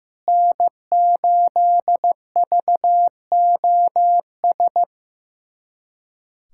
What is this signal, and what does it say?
Morse code, heard as a keyed tone, about 15 wpm, 700 Hz. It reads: N8VOS
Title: USA Callsign DE Callsign 15wpm